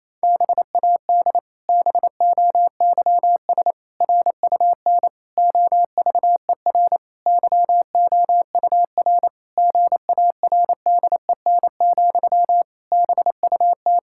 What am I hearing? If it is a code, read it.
Morse code, heard as a keyed tone, about 28 wpm, 700 Hz. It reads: BAD BOYS RUN OVER YOUR GARDEN, BUT
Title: Pangram Challenge v2 - 28wpm